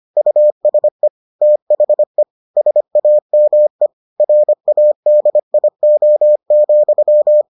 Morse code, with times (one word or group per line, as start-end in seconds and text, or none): 0.2-1.1 USE
1.4-2.2 THE
2.6-3.9 SAME
4.2-7.4 RADIO,